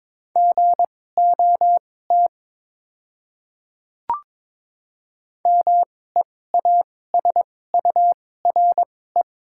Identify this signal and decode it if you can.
Morse code, heard as a keyed tone, about 22 wpm, spaced as for 16 wpm, 690 Hz. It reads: GOT MEASURE
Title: 2x Character Spacing - Top 500 Words - 22wpm